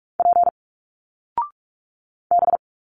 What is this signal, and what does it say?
Morse code, heard as a keyed tone, about 46 wpm, 700 Hz. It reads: P B
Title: Single Letters 45wpm